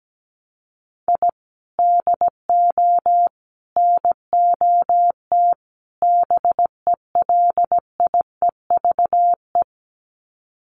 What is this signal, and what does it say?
Morse code, heard as a keyed tone, about 17 wpm, 700 Hz. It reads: I DO NOT BELIEVE
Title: Sets of 4 Words 17wpm